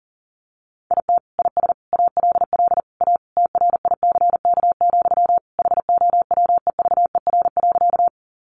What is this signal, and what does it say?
Morse code, heard as a keyed tone, about 40 wpm, 700 Hz. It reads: IT IS ALL A TRICK, HOWEVER.
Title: Sentences from Top 1500 Words 40wpm